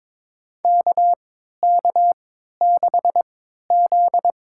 Morse code, heard as a keyed tone, about 22 wpm, spaced as for 13 wpm, 700 Hz. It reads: KK6Z
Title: USA Call Signs - 3x Character Spacing - 22wpm